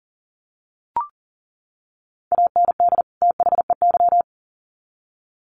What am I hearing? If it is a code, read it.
Morse code, heard as a keyed tone, about 40 wpm, 700 Hz. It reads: AND THEY
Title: Sets of 2 Words 40wpm